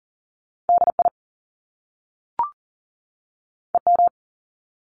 Morse code, heard as a keyed tone, about 39 wpm, 700 Hz. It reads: DI EM
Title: Top Two Letter Combinations in Words 40wpm